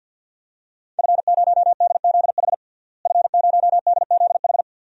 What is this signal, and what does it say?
Morse code, heard as a keyed tone, about 46 wpm, 700 Hz. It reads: F0DZH F0DZH
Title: DXCC-100 Call Signs - 5 Characters - Repeated 2x - 50wpm